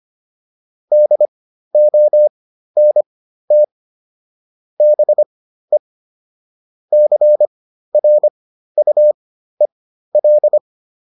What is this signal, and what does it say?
Morse code, heard as a keyed tone, about 25 wpm, spaced as for 13 wpm, 600 Hz. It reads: DONT BE CRUEL
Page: intermediate-practice-files